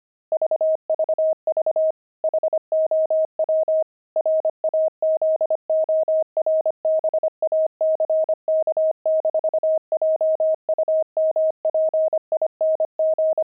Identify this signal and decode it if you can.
Morse code, heard as a keyed tone, about 25 wpm, 630 Hz. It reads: VVV HOW RAZORBACK-JUMPING
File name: pangram-fs-25-cs-25-frogs